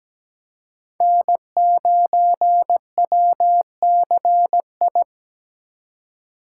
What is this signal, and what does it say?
Morse code, heard as a keyed tone, about 17 wpm, 700 Hz. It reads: N9WCI
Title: USA Call Signs - 2x Character Spacing - 17wpm